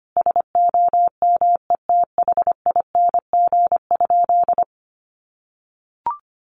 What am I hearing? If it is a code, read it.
Morse code, heard as a keyed tone, about 25 wpm, 700 Hz. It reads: SOMETHING?
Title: Sentences from Top 1100 Words 25wpm